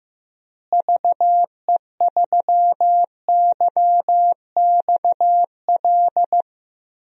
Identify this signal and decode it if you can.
Morse code, heard as a keyed tone, about 15 wpm, 700 Hz. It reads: VE3YXL
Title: DXCC-100 Call Signs - 6 Characters - Repeated 2x - 15wpm